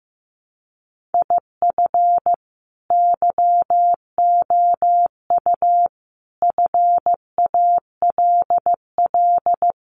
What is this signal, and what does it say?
Morse code, heard as a keyed tone, about 15 wpm, 700 Hz. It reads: IF YOU FALL
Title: A Sound Of Thunder 15wpm